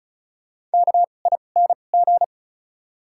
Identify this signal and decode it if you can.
Morse code, heard as a keyed tone, about 36 wpm, spaced as for 25 wpm, 700 Hz. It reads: KING
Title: 2x Character Spacing - Top 500 Words - 35wpm